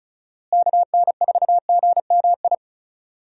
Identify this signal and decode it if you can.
Morse code, heard as a keyed tone, about 36 wpm, 700 Hz. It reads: KN4GMI
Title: USA Call Signs - 6 Characters - Repeated 3x - 35wpm